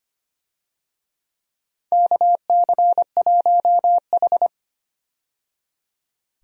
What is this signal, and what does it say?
Morse code, heard as a keyed tone, about 25 wpm, 700 Hz. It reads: KC1H
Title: USA Call Signs - 4 Characters - 25wpm